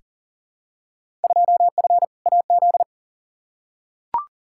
Morse code, heard as a keyed tone, about 40 wpm, 700 Hz. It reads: 2F AZ
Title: Field Day Exchanges - 40wpm